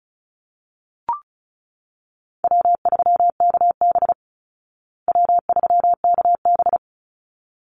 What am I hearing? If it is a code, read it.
Morse code, heard as a keyed tone, about 34 wpm, 700 Hz. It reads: W3KB W3KB
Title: Easy Call Signs Repeated 2x - 35wpm